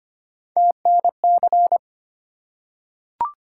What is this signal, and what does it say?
Morse code, heard as a keyed tone, about 25 wpm, 700 Hz. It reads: TNC
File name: Three Letter-Number Combinations - 25wpm